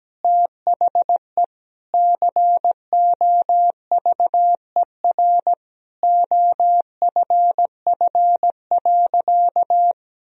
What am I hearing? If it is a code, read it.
Morse code, heard as a keyed tone, about 17 wpm, 700 Hz. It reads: THE COVER OFF.
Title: Sentences from Top 1000 Words - Review Part 1 - 17wpm